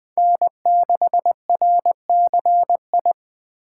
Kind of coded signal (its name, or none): Morse code